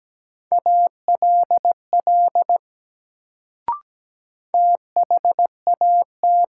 Morse code, heard as a keyed tone, about 17 wpm, 700 Hz. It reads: ALL THAT
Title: Sentences from Top 100 Words - New Word Drill - 17wpm